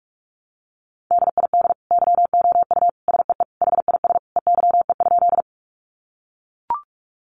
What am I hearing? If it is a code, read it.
Morse code, heard as a keyed tone, about 44 wpm, 700 Hz. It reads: DID YOU SEE HIS EYE?
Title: Sentences from Top 300 Words 45wpm